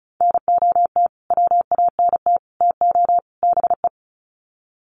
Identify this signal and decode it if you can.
Morse code, heard as a keyed tone, about 34 wpm, 700 Hz. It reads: NOT WANT TO BE
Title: Sets of 4 Words 35wpm